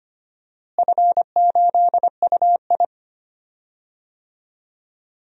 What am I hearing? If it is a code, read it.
Morse code, heard as a keyed tone, about 25 wpm, 700 Hz. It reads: F8UI
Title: DXCC-100 Call Signs - 4 Characters - Repeated 2x - 25wpm